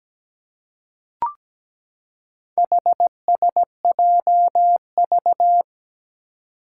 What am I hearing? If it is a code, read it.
Morse code, heard as a keyed tone, about 17 wpm, 700 Hz. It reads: HSJV
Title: Four Challenging Letter-Number Combinations 17wpm